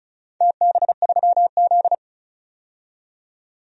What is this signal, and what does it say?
Morse code, heard as a keyed tone, about 36 wpm, 700 Hz. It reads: TB3Z
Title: DXCC-100 Call Signs - 4 Characters - 35wpm